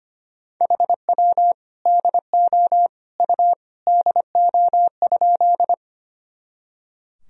Morse code, heard as a keyed tone, about 25 wpm, 700 Hz. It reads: HW DO U DO?
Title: Sentences from Top 100 Words and Abbreviations 25wpm